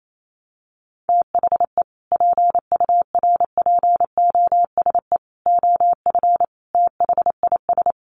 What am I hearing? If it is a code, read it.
Morse code, heard as a keyed tone, about 28 wpm, 700 Hz. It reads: THE PURPOSE OF THIS
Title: Sets of 5 Words 28wpm